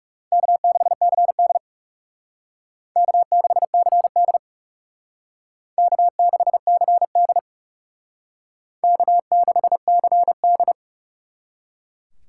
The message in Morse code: K6CD K6CD K6CD K6CD

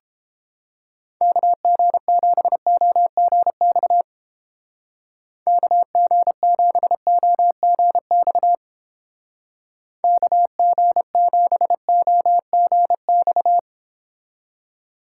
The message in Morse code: KG7OGX KG7OGX KG7OGX

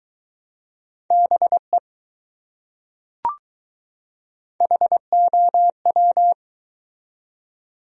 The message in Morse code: BE HOW

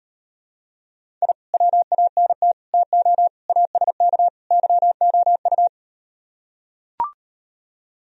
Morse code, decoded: I WANT TO ASK YOU

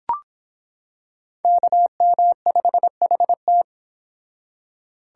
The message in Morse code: KM5HT

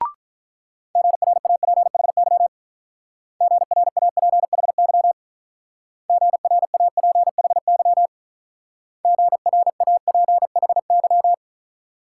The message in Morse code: GRAPHY GRAPHY GRAPHY GRAPHY

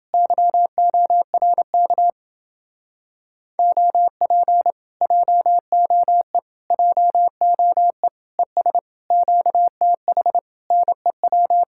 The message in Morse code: YORK OP JOE JOE ES QTH NEW

